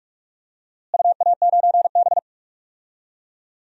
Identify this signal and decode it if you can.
Morse code, heard as a keyed tone, about 46 wpm, 700 Hz. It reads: UA9B
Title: DXCC-100 Call Signs - 4 Characters - Repeated 3x - 45wpm